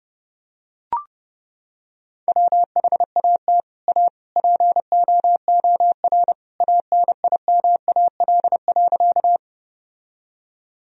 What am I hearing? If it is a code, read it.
Morse code, heard as a keyed tone, about 32 wpm, 700 Hz. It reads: WHAT A POOR ANIMAL.
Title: Sentences from Top 1000 Words - Review Part 1 - 30wpm